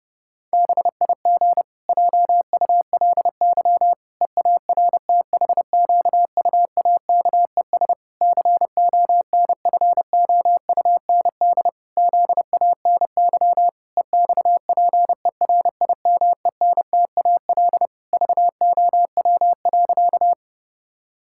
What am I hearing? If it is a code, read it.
Morse code, heard as a keyed tone, about 32 wpm, 700 Hz. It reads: BIG JULY EARTHQUAKES CONFOUND ZANY EXPERIMENTAL VOW.
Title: Pangram Challenge v2 - 30wpm